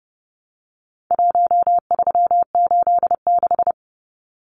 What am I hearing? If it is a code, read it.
Morse code, heard as a keyed tone, about 30 wpm, 700 Hz. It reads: 1386
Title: Numbers - Contest Style - 4-Digit 30wpm